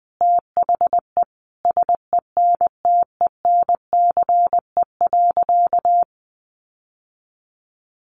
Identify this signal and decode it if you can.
Morse code, heard as a keyed tone, about 20 wpm, 700 Hz. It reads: THE SENTENCE.
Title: Sentences from Top 1200 Words 20wpm